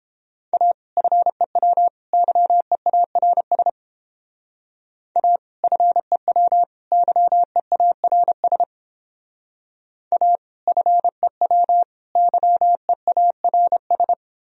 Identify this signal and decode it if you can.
Morse code, heard as a keyed tone, about 31 wpm, 700 Hz. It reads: A FEW YEARS A FEW YEARS A FEW YEARS